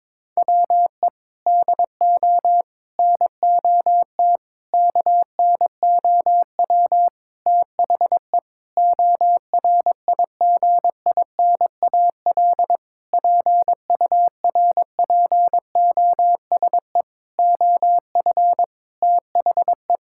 Morse code, heard as a keyed tone, about 22 wpm, 700 Hz. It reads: WE DO NOT KNOW THE ORIGINAL PURPOSE OF THE
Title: Sentences from Top 1000 Words - Review Part 6 - 22wpm